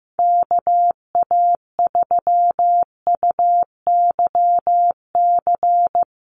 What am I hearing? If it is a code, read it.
Morse code, heard as a keyed tone, about 15 wpm, 700 Hz. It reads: KA3UYC
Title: USA Call Signs Repeated 2x 15wpm